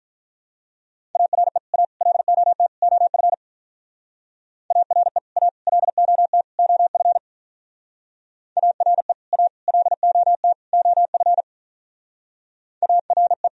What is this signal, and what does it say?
Morse code, heard as a keyed tone, about 42 wpm, 700 Hz. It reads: ARE A LOT OF ARE A LOT OF ARE A LOT OF ARE
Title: Speed-Racing - Sets of 4 Words 35wpm